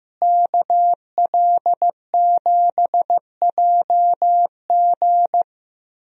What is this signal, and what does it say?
Morse code, heard as a keyed tone, about 15 wpm, 700 Hz. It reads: KL7JG
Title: DXCC-100 Call Signs - 5 Characters - Repeated 2x - 15wpm